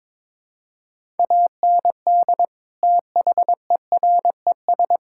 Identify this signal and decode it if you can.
Morse code, heard as a keyed tone, about 22 wpm, 700 Hz. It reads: AND THERES
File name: practice / 22wpm / Sets of 8 Words - 22wpm